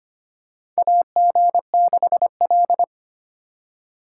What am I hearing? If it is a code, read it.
Morse code, heard as a keyed tone, about 25 wpm, 700 Hz. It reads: AG6L
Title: USA Call Signs - 4 Characters - 2x Character Spacing - 25wpm